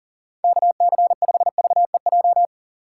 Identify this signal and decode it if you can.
Morse code, heard as a keyed tone, about 40 wpm, 700 Hz. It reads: KC5VEJ
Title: USA Call Signs - 4x Character Spacing - 40wpm